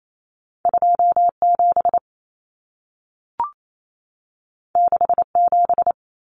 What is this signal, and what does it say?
Morse code, heard as a keyed tone, about 28 wpm, 700 Hz. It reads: 27 67
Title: Common Cut Numbers - 2-Digit 28wpm